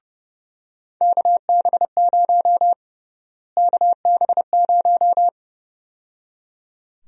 Morse code, heard as a keyed tone, about 32 wpm, 700 Hz. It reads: KB0 KB0
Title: USA Call Sign Prefixes - Repeated 2x - 30wpm